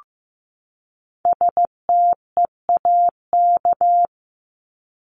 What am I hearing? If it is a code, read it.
Morse code, heard as a keyed tone, about 15 wpm, 700 Hz. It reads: STEAK
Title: Top 1251-1500 Words 15wpm